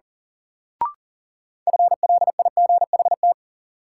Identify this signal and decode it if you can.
Morse code, heard as a keyed tone, about 37 wpm, 700 Hz. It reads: FLIGHT
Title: Sentences from Top 1000 Words - New Word Drill - 40wpm